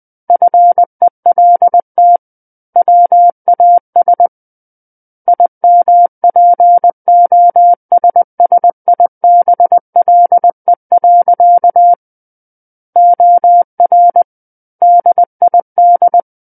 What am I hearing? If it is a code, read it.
Morse code, heard as a keyed tone, about 20 wpm, 700 Hz. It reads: FELT WAS IMPOSSIBLE. OR DID
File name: 20 WPM